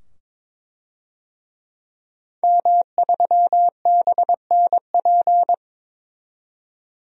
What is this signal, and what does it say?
Morse code, heard as a keyed tone, about 21 wpm, 700 Hz. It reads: M3BNP
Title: DXCC-100 Call Signs - 5 Characters - 22wpm